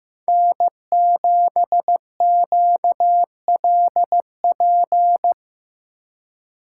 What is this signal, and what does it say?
Morse code, heard as a keyed tone, about 17 wpm, 700 Hz. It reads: N7QLP
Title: Speed Racing - USA 1x3 Call Signs 15wpm